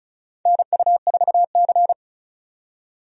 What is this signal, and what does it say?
Morse code, heard as a keyed tone, about 36 wpm, 700 Hz. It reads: NU4C
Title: USA Call Signs - 4 Characters - 35wpm